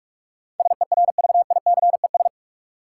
Morse code, heard as a keyed tone, about 46 wpm, 700 Hz. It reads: SERVICES
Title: Sets of 5 Words - New Word Drill - 45wpm